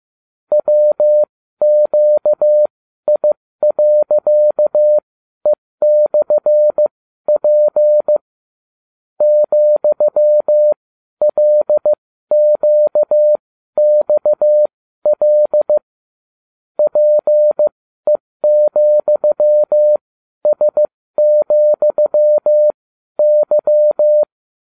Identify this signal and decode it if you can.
Morse code, heard as a keyed tone, about 15 wpm, spaced as for 12 wpm, 600 Hz. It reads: WQI.E/P ,LQXL PE,S,Y